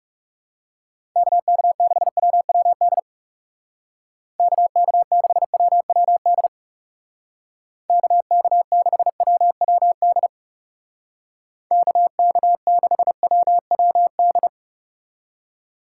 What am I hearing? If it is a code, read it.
Morse code, heard as a keyed tone, about 38 wpm, 700 Hz. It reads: KK6WWD KK6WWD KK6WWD KK6WWD